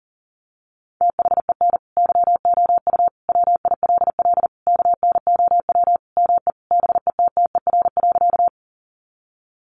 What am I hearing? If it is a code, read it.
Morse code, heard as a keyed tone, about 40 wpm, 700 Hz. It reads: THEN YOU WILL KNOW ME BETTER.
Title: Sentences from Top 400 Words 40wpm